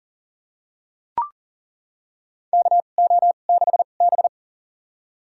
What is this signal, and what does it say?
Morse code, heard as a keyed tone, about 40 wpm, spaced as for 29 wpm, 700 Hz. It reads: KO6B
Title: USA Call Signs - 2x Character Spacing - 40wpm